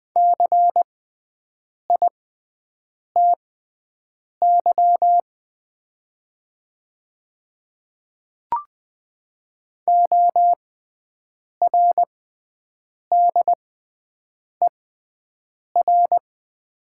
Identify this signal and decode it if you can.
Morse code, heard as a keyed tone, about 20 wpm, spaced as for 7 wpm, 700 Hz. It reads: CITY ORDER
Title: 6x Character Spacing - Top 500 Words - 20wpm